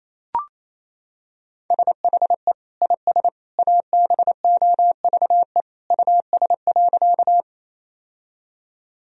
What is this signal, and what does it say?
Morse code, heard as a keyed tone, about 28 wpm, 700 Hz. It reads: SHE IS ABOVE US.
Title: Sentences from Top 1000 Words - Review Part 2 - 28wpm